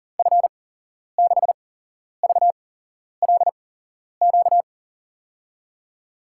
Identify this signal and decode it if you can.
Morse code, heard as a keyed tone, about 42 wpm, spaced as for 11 wpm, 700 Hz. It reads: F6VLQ
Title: USA Call Signs - 6 Characters - 8x Character Spacing - 40wpm